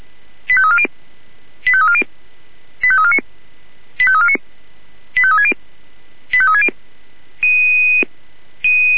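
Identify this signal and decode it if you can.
Five-tone selective call (ZVEI), heard as quick runs of five tones, tones of about 70 ms. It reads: 85340 85340 85349 85349 85360 85360
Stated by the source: ZVEI five-tone call received on SDR.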